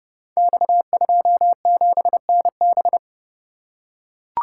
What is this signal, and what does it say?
Morse code, heard as a keyed tone, about 30 wpm, 700 Hz. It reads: X27NB
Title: Code Groups - 5 Letter:Numbers - 30wpm